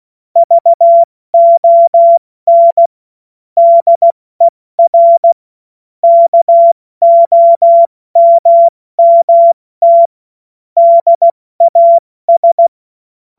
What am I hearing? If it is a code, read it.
Morse code, heard as a keyed tone, about 16 wpm, 680 Hz. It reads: VON DER KOMMT DAS
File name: Morsekurs / Witze / Witz 21. Juni 2022 70 F80